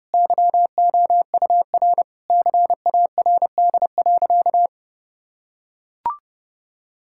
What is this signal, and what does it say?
Morse code, heard as a keyed tone, about 30 wpm, 700 Hz. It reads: YOUR CARD.
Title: Sentences from Top 1000 Words - Review Part 6 - 30wpm